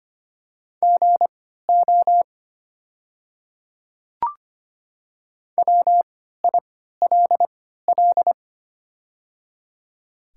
Morse code, heard as a keyed tone, about 25 wpm, spaced as for 14 wpm, 700 Hz. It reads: GO WILL
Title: 3x Character Spacing - Top 200 Words - 25wpm